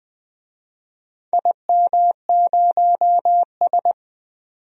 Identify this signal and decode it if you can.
Morse code, heard as a keyed tone, about 20 wpm, 700 Hz. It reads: IM0S
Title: DXCC-100 Call Signs - 4 Characters - 20wpm